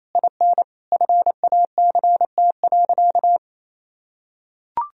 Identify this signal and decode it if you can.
Morse code, heard as a keyed tone, about 27 wpm, 700 Hz. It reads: IN FACT.
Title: Sentences from Top 1000 Words - Review Part 3 - 28wpm